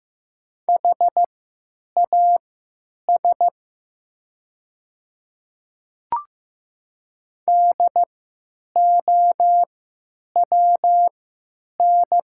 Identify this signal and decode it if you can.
Morse code, heard as a keyed tone, about 15 wpm, spaced as for 9 wpm, 700 Hz. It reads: HAS DOWN
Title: 3x Character Spacing - Top 100 Words - 15wpm